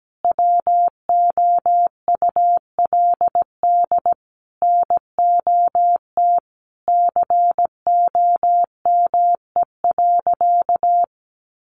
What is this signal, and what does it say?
Morse code, heard as a keyed tone, about 17 wpm, 700 Hz. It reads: WOULD NOT COME.
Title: Sentences from Top 1500 Words 17wpm